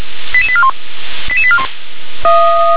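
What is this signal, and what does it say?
Five-tone selective call (ZVEI), heard as quick runs of five tones, tones of about 70 ms. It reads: 88841 88841
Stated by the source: ZVEI five-tone selective call 88981 received by SDR.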